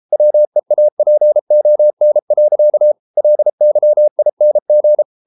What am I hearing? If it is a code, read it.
Morse code, heard as a keyed tone, about 33 wpm, 600 Hz. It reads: WEAPON. LYING